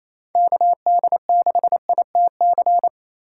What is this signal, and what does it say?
Morse code, heard as a keyed tone, about 34 wpm, 700 Hz. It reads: KD6ITC
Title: Speed Racing - USA Call Signs 28wpm